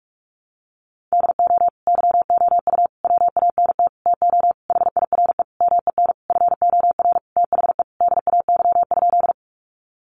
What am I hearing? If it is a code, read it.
Morse code, heard as a keyed tone, about 44 wpm, 700 Hz. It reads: DO YOU WANT TO HIRE MEN FOR THE DAY?
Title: Sentences from Top 1400 Words 45wpm